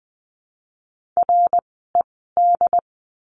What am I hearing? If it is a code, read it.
Morse code, heard as a keyed tone, about 20 wpm, spaced as for 15 wpm, 700 Hz. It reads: RED
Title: 094 Lesson 16f - Letter c - Words 20wpm